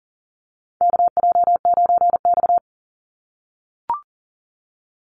Morse code, heard as a keyed tone, about 39 wpm, 700 Hz. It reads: KJ9X
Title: NA SOTA Chaser Callsigns 40wpm